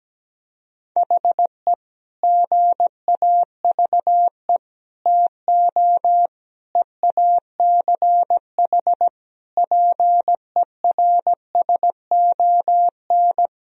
Morse code, heard as a keyed tone, about 17 wpm, 700 Hz. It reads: HE GAVE TO EACH PERSON
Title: Sentences from Top 1000 Words - Review Part 4 - 17wpm